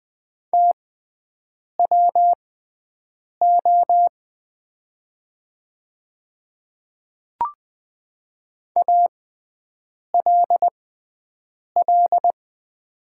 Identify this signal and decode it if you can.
Morse code, heard as a keyed tone, about 20 wpm, spaced as for 7 wpm, 700 Hz. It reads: TWO ALL
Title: 6x Character Spacing - Top 300 Words - 20wpm